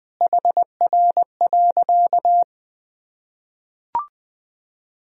Morse code, heard as a keyed tone, about 20 wpm, 700 Hz. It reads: HR.
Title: Sentences from Top 500 Words and Abbreviations 20wpm